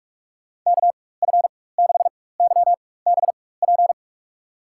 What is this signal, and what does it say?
Morse code, heard as a keyed tone, about 46 wpm, spaced as for 21 wpm, 700 Hz. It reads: KF6YBP
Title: USA Call Signs - 6 Characters - 4x Character Spacing - 45wpm